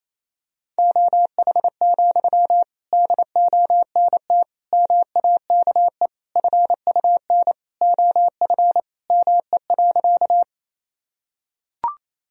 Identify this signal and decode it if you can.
Morse code, heard as a keyed tone, about 28 wpm, 700 Hz. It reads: OH, DONT MAKE FUN OF ME.
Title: Sentences from Top 1000 Words - Review Part 4 - 28wpm